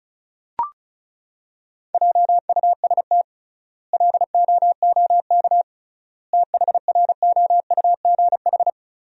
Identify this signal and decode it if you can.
Morse code, heard as a keyed tone, about 36 wpm, 700 Hz. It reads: JUST LOOK THROUGH
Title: 3x Word-Spacing - Sentences from Top 200 Words - 35wpm